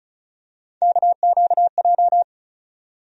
Morse code, heard as a keyed tone, about 36 wpm, 700 Hz. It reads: KQJ
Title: USA Call Sign Postfixes - 3 Letters 35wpm